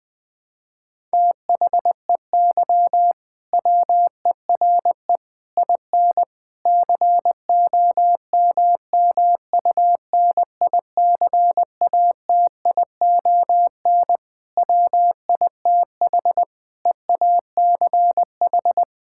Morse code, heard as a keyed tone, about 20 wpm, 700 Hz. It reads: THEY WERE IN COMMUNICATION WITH EACH
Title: Sentences from Top 1000 Words - Review Part 6 - 20wpm